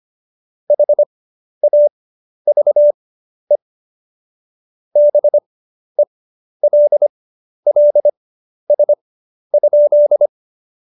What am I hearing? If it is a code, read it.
Morse code, heard as a keyed tone, about 25 wpm, spaced as for 11 wpm, 590 Hz. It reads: HAVE BELLS?
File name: Jokes-25-char-10-eff